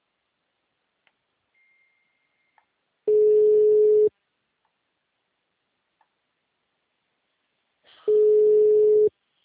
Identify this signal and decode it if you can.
Ringback tone (Europe, 425 Hz).